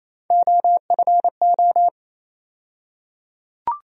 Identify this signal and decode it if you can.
Morse code, heard as a keyed tone, about 28 wpm, 700 Hz. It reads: OFO
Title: USA Call Sign Postfixes - 3 Letters 28wpm